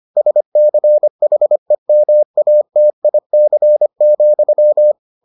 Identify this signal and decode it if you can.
Morse code, heard as a keyed tone, about 25 wpm, 600 Hz. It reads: SCHEMATIC,